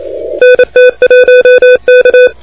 Morse code, heard as a keyed tone, about 28 wpm, 500 Hz, with noise. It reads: NT1K
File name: recordings / W2UP de NT1K ARRL10M CW 28MHz